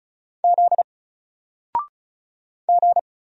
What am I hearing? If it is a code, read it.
Morse code, heard as a keyed tone, about 36 wpm, 700 Hz. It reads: Z G
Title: Single Character (Letter:Number:Punctuation:Prosign) Alt 35wpm